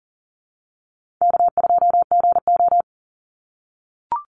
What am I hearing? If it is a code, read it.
Morse code, heard as a keyed tone, about 40 wpm, 700 Hz. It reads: K2GO
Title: Call Signs - 40wpm